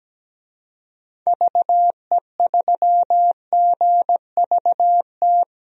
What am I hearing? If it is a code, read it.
Morse code, heard as a keyed tone, about 17 wpm, 700 Hz. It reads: VE3GVT
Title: DXCC-100 Call Signs - 6 Characters - Repeated 3x - 17wpm